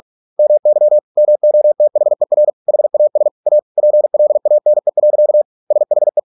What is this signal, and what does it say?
Morse code, heard as a keyed tone, about 46 wpm, 600 Hz. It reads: MY MOTHER HAS A PLANE. SHE